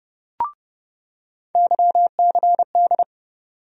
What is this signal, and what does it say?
Morse code, heard as a keyed tone, about 30 wpm, 700 Hz. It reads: YCD
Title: USA Call Sign Postfixes - 3 Letters 30wpm